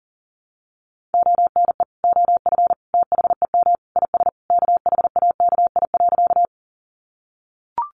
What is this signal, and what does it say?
Morse code, heard as a keyed tone, about 37 wpm, 700 Hz. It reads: ONE OF THEM IS KHAKI.
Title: Additional Easy Phrases 40wpm